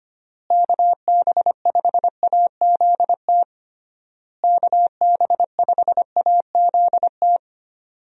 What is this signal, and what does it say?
Morse code, heard as a keyed tone, about 25 wpm, 700 Hz. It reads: KB5AZT KB5AZT
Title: USA Call Signs - 6 Characters - Repeated 3x - 25wpm